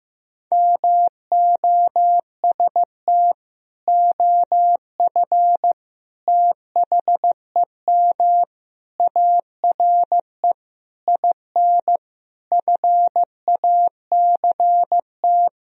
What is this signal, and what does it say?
Morse code, heard as a keyed tone, about 15 wpm, 700 Hz. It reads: MOST OF THEM ARE IN FACT
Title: Sentences from Top 1000 Words - Review Part 4 - 15wpm